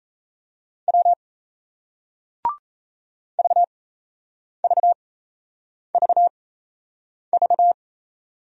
Morse code, heard as a keyed tone, about 39 wpm, spaced as for 8 wpm, 700 Hz. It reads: W VVVV